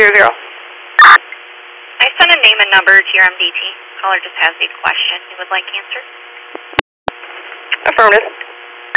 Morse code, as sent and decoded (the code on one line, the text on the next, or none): none